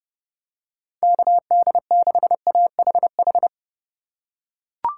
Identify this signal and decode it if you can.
Morse code, heard as a keyed tone, about 31 wpm, 700 Hz. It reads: KD6AHH
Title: USA Call Signs - 6 Characters - 3x Character Spacing - 30wpm